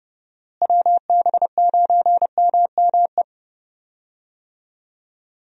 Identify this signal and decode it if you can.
Morse code, heard as a keyed tone, about 30 wpm, 700 Hz. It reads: WB9MME
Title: USA Call Signs - 6 Characters - Repeated 3x - 30wpm